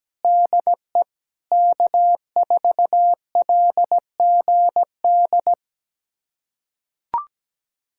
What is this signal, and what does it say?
Morse code, heard as a keyed tone, about 17 wpm, 700 Hz. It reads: DE K4LGD
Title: USA Callsign DE Callsign - 5 Characters - 17wpm